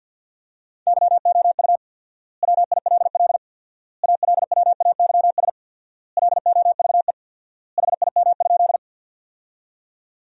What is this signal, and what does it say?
Morse code, heard as a keyed tone, about 53 wpm, 700 Hz. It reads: YOU WILL ALWAYS LOVE HIM?
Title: 4x Word-Spacing - Sentences from Top 400 Words - 50wpm